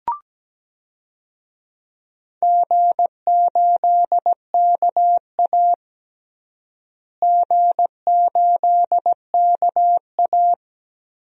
Morse code, heard as a keyed tone, about 17 wpm, 700 Hz. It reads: G8KA G8KA
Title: DXCC-100 Call Signs - 4 Characters - Repeated 2x - 17wpm